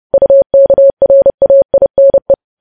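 Morse code, heard as a keyed tone, about 30 wpm, 550 Hz. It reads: UKRAINE
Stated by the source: Morse news headlines sent at 30 wpm, opening with vvv.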